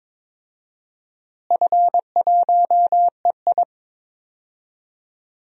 Morse code, heard as a keyed tone, about 22 wpm, 700 Hz. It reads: F1EI
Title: DXCC-100 Call Signs - 4 Characters - Repeated 2x - 22wpm